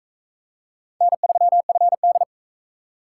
Morse code, heard as a keyed tone, about 41 wpm, 700 Hz. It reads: N3FD